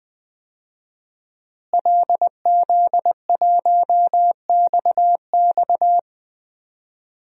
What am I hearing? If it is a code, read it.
Morse code, heard as a keyed tone, about 20 wpm, 700 Hz. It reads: LZ1XX
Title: DXCC-100 Call Signs - 5 Characters - Repeated 2x - 20wpm